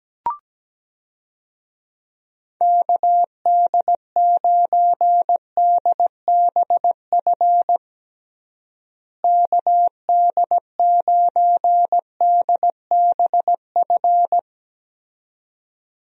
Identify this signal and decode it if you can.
Morse code, heard as a keyed tone, about 17 wpm, 700 Hz. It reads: KD9DBF KD9DBF
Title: USA Call Signs - 6 Characters - Repeated 2x - 17wpm